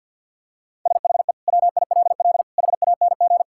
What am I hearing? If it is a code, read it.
Morse code, heard as a keyed tone, about 50 wpm, 700 Hz. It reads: SHE WILL HANG
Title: Sentences from Top 1600 Words 50wpm